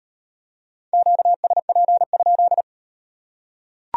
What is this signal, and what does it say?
Morse code, heard as a keyed tone, about 38 wpm, 700 Hz. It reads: QSP?